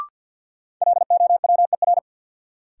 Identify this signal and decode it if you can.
Morse code, heard as a keyed tone, about 50 wpm, 700 Hz. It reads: LOWER
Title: Sentences from Top 700 Words - New Word Drill - 50wpm